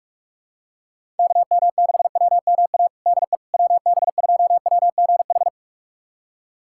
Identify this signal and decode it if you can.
Morse code, heard as a keyed tone, about 44 wpm, 700 Hz. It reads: KM6WMA DE WB2WGH
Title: USA Callsign DE Callsign - 6 Characters - 45wpm